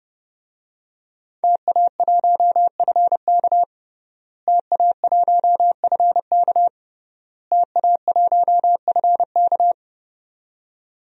Morse code, heard as a keyed tone, about 30 wpm, 700 Hz. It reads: TA1FK TA1FK TA1FK
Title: DXCC-100 Call Signs - 5 Characters - Repeated 3x - 30wpm